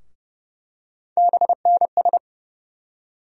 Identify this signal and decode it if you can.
Morse code, heard as a keyed tone, about 30 wpm, 700 Hz. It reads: BNS
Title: USA Call Sign Postfixes - 3 Letters 30wpm